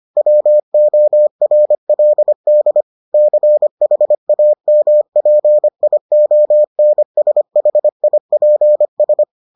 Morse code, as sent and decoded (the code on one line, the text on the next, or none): .-- --- .-. .-.. -.. / -.-. .... .- -- .--. .. --- -. ... .... .. .--. ...
WORLD CHAMPIONSHIPS